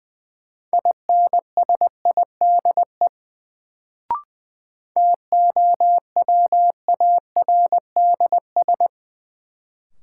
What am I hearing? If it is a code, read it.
Morse code, heard as a keyed tone, about 20 wpm, 700 Hz. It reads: INSIDE TOWARDS
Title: Sentences from Top 1000 Words - Review Part 2 - New Word Drill - 20wpm